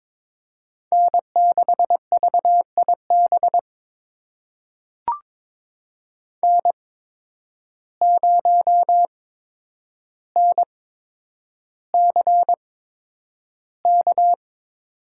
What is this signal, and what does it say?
Morse code, heard as a keyed tone, about 22 wpm, spaced as for 9 wpm, 700 Hz. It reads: N6VIB N0NCK
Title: USA Call Signs - 5 Characters - 8x Character Spacing - 22wpm